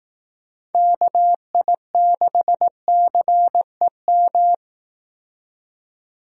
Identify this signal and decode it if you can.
Morse code, heard as a keyed tone, about 18 wpm, 700 Hz. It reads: KI6CEM
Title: Speed Racing - USA 2x3 Call Signs 15wpm